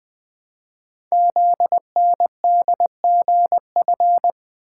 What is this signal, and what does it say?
Morse code, heard as a keyed tone, about 20 wpm, 700 Hz. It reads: ZNDGF
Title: Code Groups - 5 Letters - 20wpm